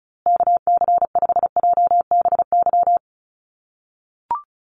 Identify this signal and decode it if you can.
Morse code, heard as a keyed tone, about 34 wpm, 700 Hz. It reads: KC5JBY
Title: USA Call Signs Repeated 3x 35wpm